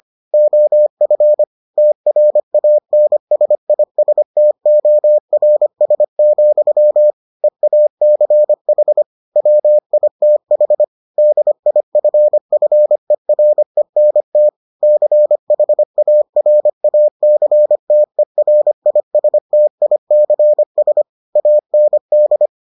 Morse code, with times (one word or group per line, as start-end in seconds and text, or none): 0.3-1.4 OF
1.8-7.1 TRANSISTORS,
7.4-9.0 EACH
9.4-10.8 WITH
11.2-14.5 DIFFERENT
14.8-21.0 CHARACTERISTICS
21.4-22.5 AND